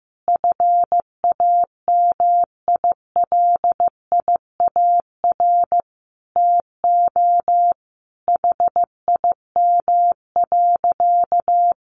Morse code, heard as a keyed tone, about 15 wpm, 700 Hz. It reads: FAMILIAR TO HIM.
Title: Sentences from Top 1000 Words 15wpm